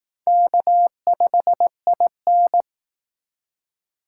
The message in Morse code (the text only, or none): K5IN